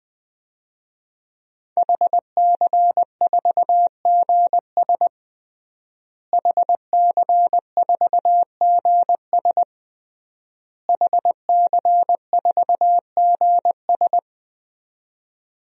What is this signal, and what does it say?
Morse code, heard as a keyed tone, about 21 wpm, 700 Hz. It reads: HC4GS HC4GS HC4GS
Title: DXCC-100 Call Signs - 5 Characters - Repeated 3x - 20wpm